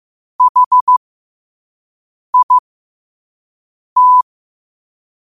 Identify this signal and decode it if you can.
Morse code, heard as a keyed tone, about 15 wpm, spaced as for 5 wpm, 1000 Hz. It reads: HIT
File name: qotdfiles / 5wpm